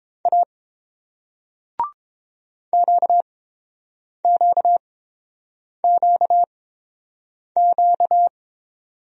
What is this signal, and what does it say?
Morse code, heard as a keyed tone, about 31 wpm, spaced as for 7 wpm, 700 Hz. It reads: A QQQQ